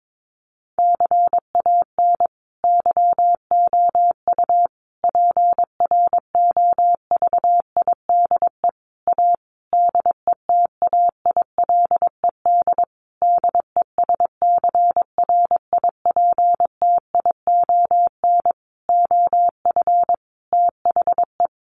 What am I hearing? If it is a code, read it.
Morse code, heard as a keyed tone, about 22 wpm, 700 Hz. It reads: CAN YOU PROVIDE A DETAILED DESCRIPTION OF THE
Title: Sentences from Top 1500 Words 22wpm